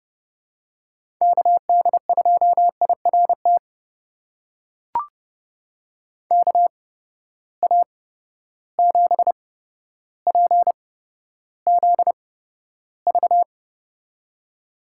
Morse code, heard as a keyed tone, about 31 wpm, spaced as for 13 wpm, 700 Hz. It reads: KD2IRT KA7PZV
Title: USA Call Signs - 6 Characters - 8x Character Spacing - 30wpm